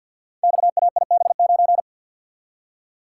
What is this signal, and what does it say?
Morse code, heard as a keyed tone, about 47 wpm, 700 Hz. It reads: XAIB9
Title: Code Groups - 5 Letter:Numbers - 50wpm